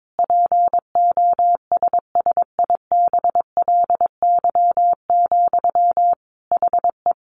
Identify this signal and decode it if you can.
Morse code, heard as a keyed tone, about 22 wpm, 700 Hz. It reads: POSSIBLY, HE
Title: Sentences from Top 1300 Words 22wpm